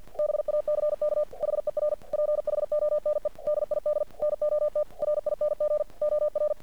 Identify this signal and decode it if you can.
Morse code, heard as a keyed tone, about 50 wpm, 600 Hz. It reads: BACK HER PHONE DID NOT RING OR
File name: archived / CRSnet